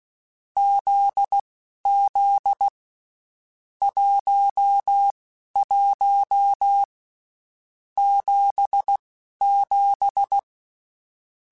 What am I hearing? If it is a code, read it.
Morse code, heard as a keyed tone, about 16 wpm, spaced as for 11 wpm, 800 Hz. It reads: ZZ 11 77
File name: Basic Sets / Basic Letter Groups Collective7 / JZ17@10